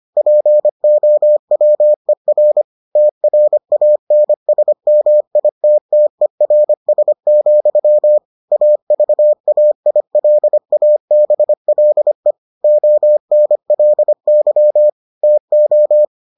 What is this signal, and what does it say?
Morse code, heard as a keyed tone, about 25 wpm, 600 Hz. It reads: POWER TRANSMITTERS, AVAILABLE ONLY TO